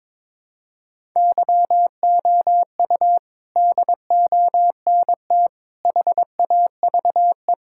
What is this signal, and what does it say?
Morse code, heard as a keyed tone, about 22 wpm, 700 Hz. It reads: YOU DONT HAVE
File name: practice / 22wpm / Sets of 6 Words - 22wpm